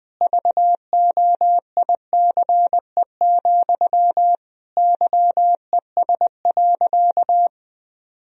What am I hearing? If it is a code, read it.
Morse code, heard as a keyed tone, about 20 wpm, 700 Hz. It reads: VOICE, YES.
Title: Sentences from Top 1000 Words - Review Part 1 - 20wpm